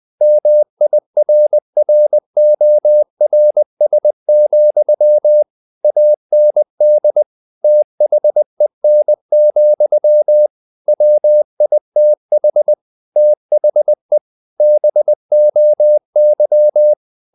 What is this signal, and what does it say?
Morse code, heard as a keyed tone, about 20 wpm, 600 Hz. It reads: MIRRORS, AND THEN, WITH THE BOY
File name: the old man and the sea / the old man and the sea part 1 20wpm